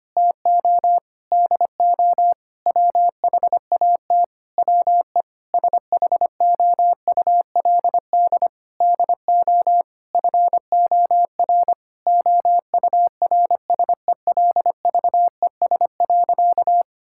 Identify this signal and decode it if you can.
Morse code, heard as a keyed tone, about 25 wpm, 700 Hz. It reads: TO DO WHAT WE SHOULD DO FOR OURSELVES.
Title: Sentences from Top 1600 Words 25wpm